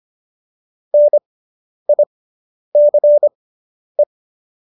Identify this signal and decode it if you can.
Morse code, heard as a keyed tone, about 25 wpm, spaced as for 10 wpm, 590 Hz. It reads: NICE